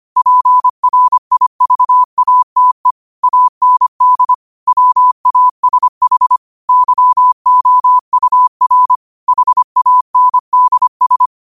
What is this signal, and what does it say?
Morse code, heard as a keyed tone, about 25 wpm, 1000 Hz. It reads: PRIVATE AND WASH YOUR HANDS
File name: qotdfiles / 25wpm